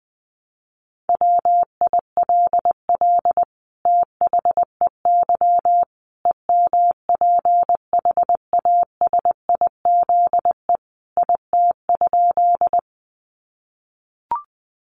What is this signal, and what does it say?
Morse code, heard as a keyed tone, about 20 wpm, 700 Hz. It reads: WILL THEY EMPHASIZE IT?
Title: Sentences from Top 1500 Words 20wpm